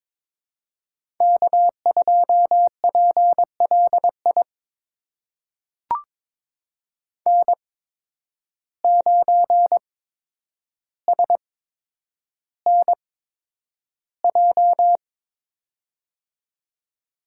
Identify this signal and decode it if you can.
Morse code, heard as a keyed tone, about 22 wpm, spaced as for 9 wpm, 700 Hz. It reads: K2PLI N9SNJ
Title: USA Call Signs - 5 Characters - 8x Character Spacing - 22wpm